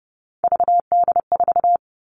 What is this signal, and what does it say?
Morse code, heard as a keyed tone, about 30 wpm, 700 Hz. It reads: VD4
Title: Three Challenging Letter-Number Combinations 30wpm